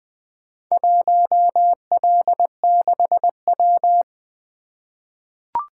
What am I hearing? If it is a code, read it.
Morse code, heard as a keyed tone, about 20 wpm, 700 Hz. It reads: 1L6W
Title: Four Letter-Number Combinations - 20wpm